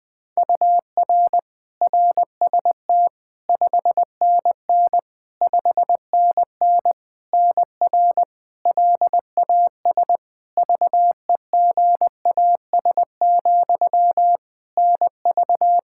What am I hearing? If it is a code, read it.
Morse code, heard as a keyed tone, about 20 wpm, 700 Hz. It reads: UR RST 5NN 5NN NR LAS VEGAS, NV
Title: K1USN SST QSO Practice 20wpm